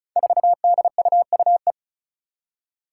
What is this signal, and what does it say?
Morse code, heard as a keyed tone, about 36 wpm, 700 Hz. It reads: 4DUUE
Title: Five Challenging Letter-Number Combinations 35wpm